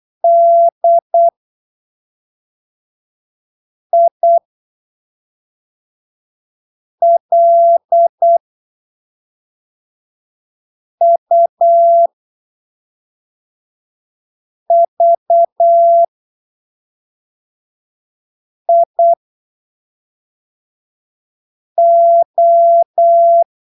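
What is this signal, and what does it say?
Morse code, heard as a keyed tone, about 8 wpm, spaced as for 3 wpm, 680 Hz. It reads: DILUVIO
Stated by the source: Morse training text at 8 ppm for Brazilian class B amateur radio telegraphy exams, from Portal do CW, PY2BRL.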